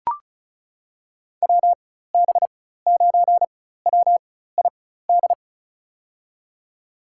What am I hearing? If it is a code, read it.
Morse code, heard as a keyed tone, about 34 wpm, spaced as for 16 wpm, 700 Hz. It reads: WB9WID
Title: USA Call Signs - 6 Characters - 4x Character Spacing - 35wpm